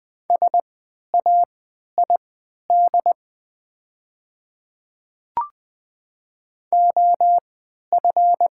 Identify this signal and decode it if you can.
Morse code, heard as a keyed tone, about 20 wpm, spaced as for 11 wpm, 700 Hz. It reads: SAID OF
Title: 3x Character Spacing - Top 100 Words - 20wpm